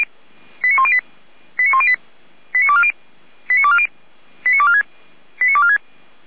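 Five-tone selective call (ZVEI), heard as quick runs of five tones, tones of about 70 ms. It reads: 89108 89108 89240 89240 89246 89246
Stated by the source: ZVEI five-tone selective call 89337 received via SDR.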